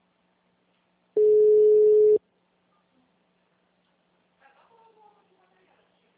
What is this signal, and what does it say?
Ringback tone (Europe, 425 Hz), with speech.